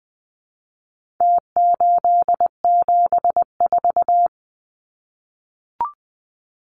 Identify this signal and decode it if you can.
Morse code, heard as a keyed tone, about 20 wpm, 700 Hz. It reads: T874
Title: Numbers - Contest Style - 4-Digit 20wpm